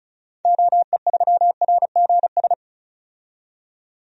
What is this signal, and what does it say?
Morse code, heard as a keyed tone, about 34 wpm, 700 Hz. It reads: OE3RGS
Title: DXCC-100 Call Signs - 6 Characters - 35wpm